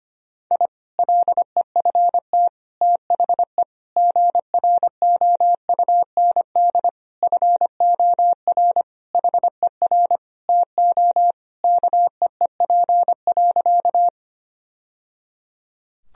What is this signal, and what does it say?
Morse code, heard as a keyed tone, about 25 wpm, 700 Hz. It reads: I LEFT THE GROUND FOR HER TO KEEP.
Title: Sentences from Top 1000 Words - Review Part 2 - 25wpm